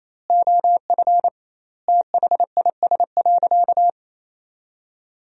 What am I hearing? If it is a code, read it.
Morse code, heard as a keyed tone, about 28 wpm, 700 Hz. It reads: OF THIS.
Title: 2x Word-Spacing - Sentences from Top 400 Words - 28wpm